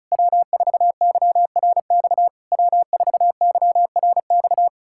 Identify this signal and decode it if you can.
Morse code, heard as a keyed tone, about 37 wpm, 700 Hz. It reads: W4YRX W4YRX
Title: Speed-Racing Common QSO Phrases - 30wpm